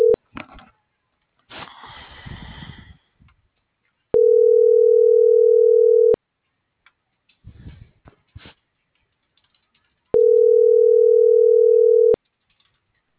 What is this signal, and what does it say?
Ringback tone (North America), 440 + 480 Hz.